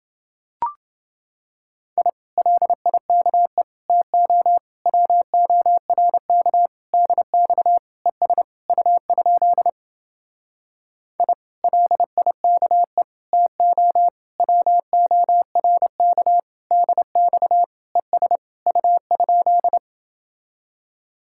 Morse code, heard as a keyed tone, about 30 wpm, 700 Hz. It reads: I LIKE TO WORK DX ES U? I LIKE TO WORK DX ES U?